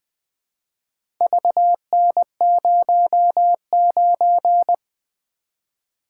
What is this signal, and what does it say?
Morse code, heard as a keyed tone, about 20 wpm, 700 Hz. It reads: VN09
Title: Four Letter-Number Combinations - 20wpm